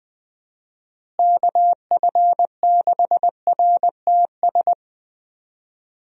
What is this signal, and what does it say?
Morse code, heard as a keyed tone, about 20 wpm, 700 Hz. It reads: KF6RTS
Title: USA Call Signs - 6 Characters - Repeated 2x - 20wpm